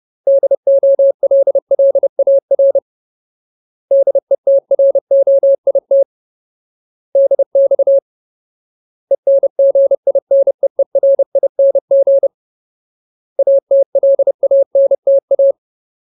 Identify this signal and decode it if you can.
Morse code, heard as a keyed tone, about 30 wpm, 550 Hz. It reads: DOLLAR DETROIT DX ENGINEERING ATLANTA
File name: Ultimate Word List / 30WPM Ultimate